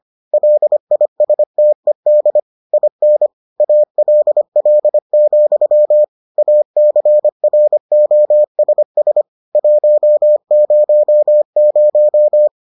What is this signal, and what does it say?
Morse code, heard as a keyed tone, about 25 wpm, 600 Hz. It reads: LISTED IN ALL, ACROSS 100